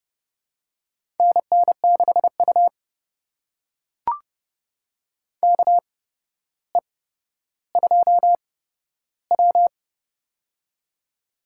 Morse code, heard as a keyed tone, about 31 wpm, spaced as for 13 wpm, 700 Hz. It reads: NN6U KE2W
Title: USA Call Signs - 4 Characters - 8x Character Spacing - 30wpm